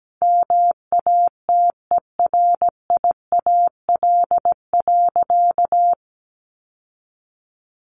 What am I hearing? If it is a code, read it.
Morse code, heard as a keyed tone, about 17 wpm, 700 Hz. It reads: MATERIAL.
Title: Sentences from Top 1100 Words 17wpm